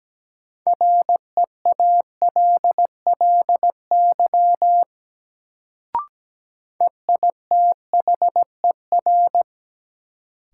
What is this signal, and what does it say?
Morse code, heard as a keyed tone, about 17 wpm, 700 Hz. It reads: REALLY EITHER
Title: Sentences from Top 600 Words - New Word Drill - 17wpm